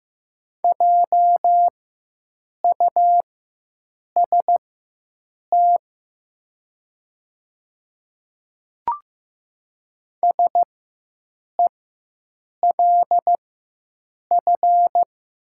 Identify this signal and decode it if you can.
Morse code, heard as a keyed tone, about 15 wpm, spaced as for 7 wpm, 700 Hz. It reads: JUST SELF
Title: 4x Character Spacing - Top 200 Words - 15wpm